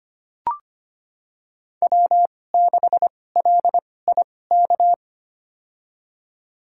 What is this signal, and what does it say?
Morse code, heard as a keyed tone, about 25 wpm, spaced as for 18 wpm, 700 Hz. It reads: W6LIK
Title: USA Call Signs - 5 Characters - 2x Character Spacing - 25wpm